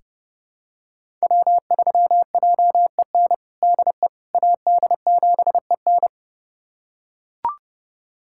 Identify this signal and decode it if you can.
Morse code, heard as a keyed tone, about 30 wpm, 700 Hz. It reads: W3JEN DE AD7EN
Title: USA Callsign DE Callsign - 5 Characters - 30wpm